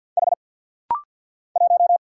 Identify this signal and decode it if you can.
Morse code, heard as a keyed tone, about 49 wpm, 700 Hz. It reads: H 1
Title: Single Letter-Number Alt - Rapid-Fire 50wpm